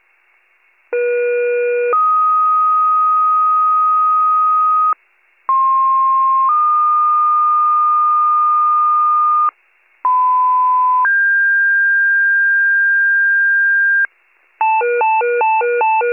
Two-tone page, heard as a one-second tone, then a three-second tone, about 480 then 1250 Hz, about 1060 then 1220 Hz, about 1010 then 1600 Hz. It is heard from 0.9 to 14.1 s.